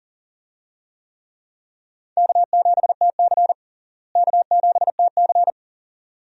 Morse code, heard as a keyed tone, about 40 wpm, 700 Hz. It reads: K7TC K7TC
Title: USA Call Signs - 4 Characters - Repeated 2x - 40wpm